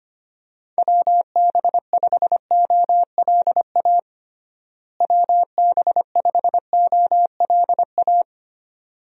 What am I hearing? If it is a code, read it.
Morse code, heard as a keyed tone, about 25 wpm, 700 Hz. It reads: WB5OLA WB5OLA
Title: USA Call Signs - 6 Characters - Repeated 2x - 25wpm